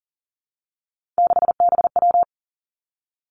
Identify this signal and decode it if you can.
Morse code, heard as a keyed tone, about 42 wpm, 700 Hz. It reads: 6BW
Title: Three Challenging Letter-Number Combinations 40wpm